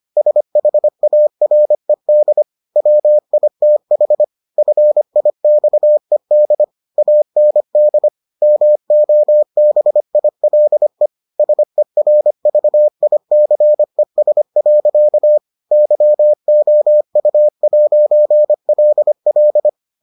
Morse code, heard as a keyed tone, about 25 wpm, 600 Hz. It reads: SHARED WITH FIXED AND MOBILE SERVICES. YOU'LL